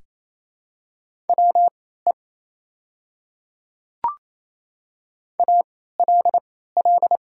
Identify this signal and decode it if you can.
Morse code, heard as a keyed tone, about 28 wpm, spaced as for 16 wpm, 700 Hz. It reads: WE ALL
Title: 3x Character Spacing - Top 300 Words - 28wpm